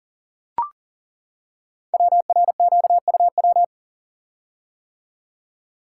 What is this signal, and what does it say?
Morse code, heard as a keyed tone, about 41 wpm, 700 Hz. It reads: WRQUW
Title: Code Groups - 5 Letters - 40wpm